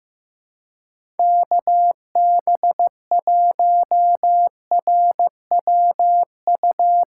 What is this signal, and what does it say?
Morse code, heard as a keyed tone, about 15 wpm, 700 Hz. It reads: KB1RWU
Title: USA Call Signs - 6 Characters - 4x Character Spacing - 15wpm